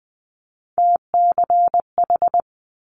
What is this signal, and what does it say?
Morse code, heard as a keyed tone, about 20 wpm, 700 Hz. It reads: TCH
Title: Multi-letter Phonemes Quick Fire 20wpm